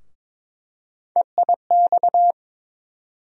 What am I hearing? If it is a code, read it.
Morse code, heard as a keyed tone, about 22 wpm, 700 Hz. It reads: EIX
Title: USA Call Sign Postfixes - 3 Letters 22wpm